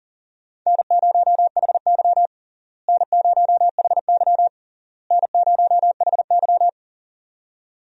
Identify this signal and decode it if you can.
Morse code, heard as a keyed tone, about 37 wpm, 700 Hz. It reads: N0HY N0HY N0HY
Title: USA Call Signs - 4 Characters - Repeated 3x - 40wpm